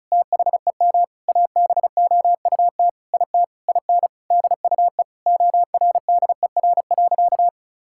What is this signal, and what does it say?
Morse code, heard as a keyed tone, about 36 wpm, 700 Hz. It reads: THEM ABOUT IT IN DUE ORDER.
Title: Sentences from Top 1000 Words - Review Part 6 - 35wpm